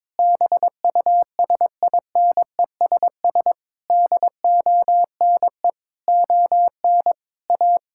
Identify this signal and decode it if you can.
Morse code, heard as a keyed tone, about 22 wpm, 700 Hz. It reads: BUSINESS DONE ON A
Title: Sentences from Top 1000 Words - Review Part 5 - 22wpm